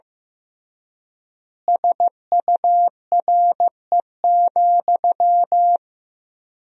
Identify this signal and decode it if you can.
Morse code, heard as a keyed tone, about 15 wpm, 700 Hz. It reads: SURE,
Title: 3x Word-Spacing - Sentences from Top 500 Words - 15wpm